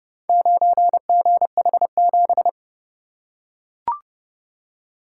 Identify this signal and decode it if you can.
Morse code, heard as a keyed tone, about 28 wpm, 700 Hz. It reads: 9GH7